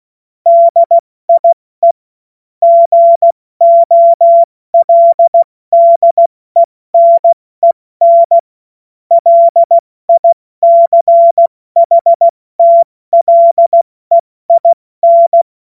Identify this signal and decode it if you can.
Morse code, heard as a keyed tone, about 16 wpm, 680 Hz. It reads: DIE GOLDENEN LICHTLEIN
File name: Morsekurs / Witze / Witz 31. Januar 2023 70 F80